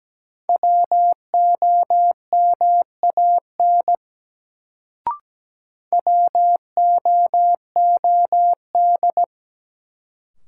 Morse code, heard as a keyed tone, about 17 wpm, 700 Hz. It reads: WOMAN WOOD
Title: Sentences from Top 1000 Words - Review Part 2 - New Word Drill - 17wpm